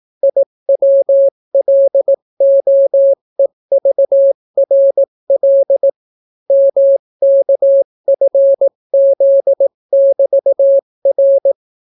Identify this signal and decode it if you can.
Morse code, heard as a keyed tone, about 18 wpm, 550 Hz. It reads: IWLOEVRL MKFZ<BT>R